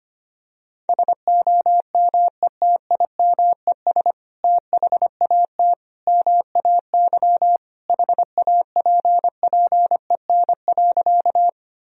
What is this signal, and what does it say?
Morse code, heard as a keyed tone, about 25 wpm, 700 Hz. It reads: SOMETIMES THAT MAY HAPPEN.
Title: Sentences from Top 1000 Words - Review Part 3 - 25wpm